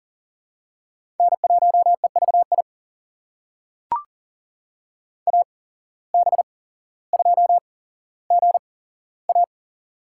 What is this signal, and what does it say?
Morse code, heard as a keyed tone, about 40 wpm, spaced as for 17 wpm, 700 Hz. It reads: N1EVI AB2GA
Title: USA Call Signs - 5 Characters - 8x Character Spacing - 40wpm